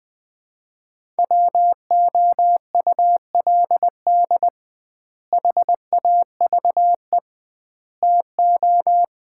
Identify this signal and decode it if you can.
Morse code, heard as a keyed tone, about 20 wpm, 700 Hz. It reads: WOULD HAVE TO
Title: 2x Word-Spacing - Sets of 3 Words - 20wpm